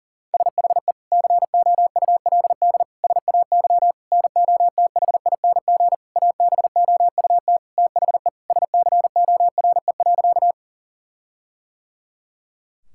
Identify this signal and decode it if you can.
Morse code, heard as a keyed tone, about 40 wpm, 700 Hz. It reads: SHE COULD SAY NOTHING ABOUT THE SCORE.
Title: Sentences from Top 1600 Words 40wpm